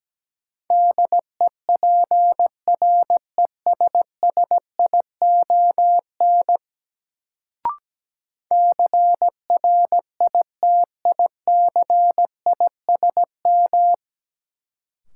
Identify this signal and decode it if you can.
Morse code, heard as a keyed tone, about 17 wpm, 700 Hz. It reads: DEPRESSION CRITICISM
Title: Sentences from Top 1300 Words - New Word Drill - 17wpm